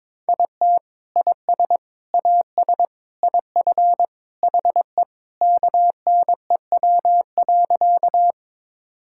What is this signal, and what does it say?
Morse code, heard as a keyed tone, about 22 wpm, 700 Hz. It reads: IT IS AS IF HE KNEW.
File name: practice / 22wpm / Sentences with 3-Word Phrases - 22wpm